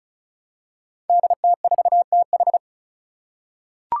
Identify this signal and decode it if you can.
Morse code, heard as a keyed tone, about 36 wpm, 700 Hz. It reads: DT4TH
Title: Five Challenging Letter-Number Combinations 35wpm